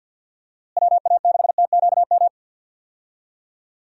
Morse code, heard as a keyed tone, about 50 wpm, 700 Hz. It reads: WA6TQM
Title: USA Call Signs - 6 Characters - Repeated 2x - 50wpm